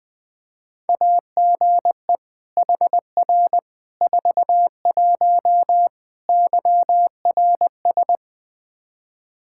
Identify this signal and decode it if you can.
Morse code, heard as a keyed tone, about 21 wpm, 700 Hz. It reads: AGE HR 41 YRS
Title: Practice QSO - Exchange 3 - More Info - 20wpm